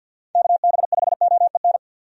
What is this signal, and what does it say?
Morse code, heard as a keyed tone, about 48 wpm, 700 Hz. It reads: KB5OEN
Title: USA Call Signs - 6 Characters - 50wpm